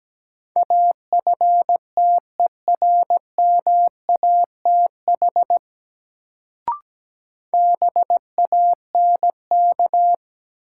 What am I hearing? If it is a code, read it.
Morse code, heard as a keyed tone, about 17 wpm, 700 Hz. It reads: AFTERMATH BANK
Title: Sets of 5 Words - New Word Drill - 17wpm